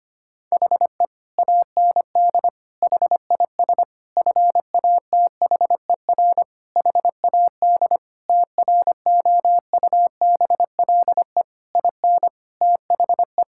Morse code, heard as a keyed tone, about 25 wpm, 700 Hz. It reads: HE AND HIS FATHER HAD TROUBLE IN THE
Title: Sentences from Top 1600 Words 25wpm